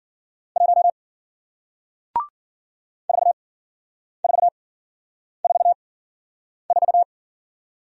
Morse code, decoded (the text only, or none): 1 4444